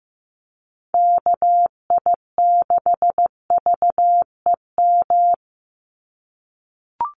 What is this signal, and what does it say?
Morse code, heard as a keyed tone, about 15 wpm, 700 Hz. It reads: KI6VEM
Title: USA Call Signs 15wpm